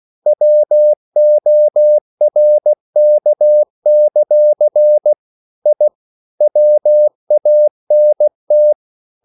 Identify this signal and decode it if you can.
Morse code, heard as a keyed tone, about 16 wpm, 600 Hz. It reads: WORK; I WANT
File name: CW / Practice / EnglishQuotations0416